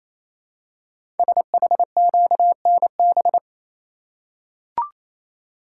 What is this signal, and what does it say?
Morse code, heard as a keyed tone, about 27 wpm, 700 Hz. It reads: SHQNB
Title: Code Groups - 5 Letters - 28wpm